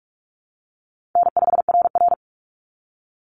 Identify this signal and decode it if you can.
Morse code, heard as a keyed tone, about 44 wpm, 700 Hz. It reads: N5RR
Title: Easy Call Signs Repeated 3x - 45wpm